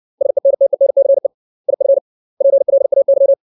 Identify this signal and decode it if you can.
Morse code, heard as a keyed tone, about 60 wpm, 550 Hz. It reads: SENTENCE IF PLAY
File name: 632 words / 632 Words 60WPM